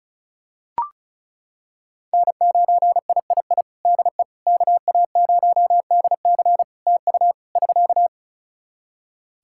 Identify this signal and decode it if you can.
Morse code, heard as a keyed tone, about 36 wpm, 700 Hz. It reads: N9III DE KA0DC TU <SK>
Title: Practice QSO - Exchange 4 - Ending - 35wpm